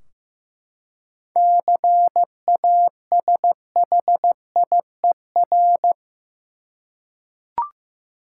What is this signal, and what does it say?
Morse code, heard as a keyed tone, about 15 wpm, 700 Hz. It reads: CASHIER
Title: Practice QSO - Exchange 3 - More Info - 15wpm